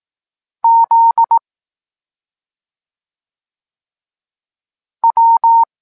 Morse code, heard as a keyed tone, about 18 wpm, 920 Hz. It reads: Z W
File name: Tekst-23 4